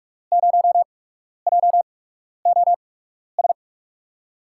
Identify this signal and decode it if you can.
Morse code, heard as a keyed tone, about 43 wpm, spaced as for 12 wpm, 700 Hz. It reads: 0JOS
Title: USA Call Signs - 6 Characters - 8x Character Spacing - 45wpm